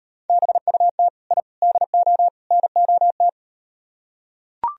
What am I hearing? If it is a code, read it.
Morse code, heard as a keyed tone, about 38 wpm, 700 Hz. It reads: BUT I DO NOT